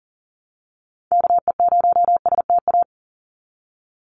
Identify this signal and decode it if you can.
Morse code, heard as a keyed tone, about 40 wpm, 700 Hz. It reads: KE0STA
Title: USA Call Signs 40wpm